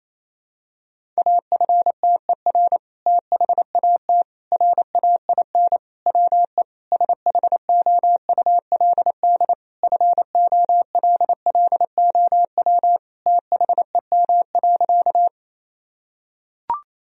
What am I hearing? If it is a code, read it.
Morse code, heard as a keyed tone, about 28 wpm, 700 Hz. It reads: AFTER THAT RAIN WE SHOULD FOLLOW THEM.
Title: Sentences from Top 1000 Words - Review Part 1 - 28wpm